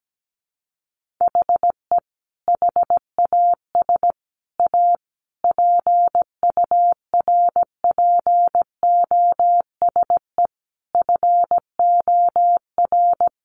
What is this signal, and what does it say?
Morse code, heard as a keyed tone, about 17 wpm, 700 Hz. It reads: HE HAS A PURPOSE FOR
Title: Sentences from Top 800 Words 17wpm